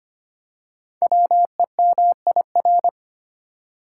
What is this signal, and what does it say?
Morse code, heard as a keyed tone, about 25 wpm, 700 Hz. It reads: WEMIR
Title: Code Groups - 5 Letters - 25wpm